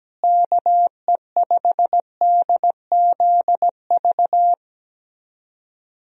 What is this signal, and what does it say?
Morse code, heard as a keyed tone, about 17 wpm, 700 Hz. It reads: KE5DZV
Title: USA Call Signs - 6 Characters - Repeated 2x - 17wpm